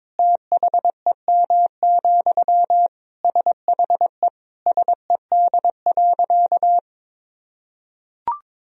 Morse code, heard as a keyed tone, about 22 wpm, 700 Hz. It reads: THEM, SHE SED.
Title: Sentences from Top 400 Words and Abbreviations 22wpm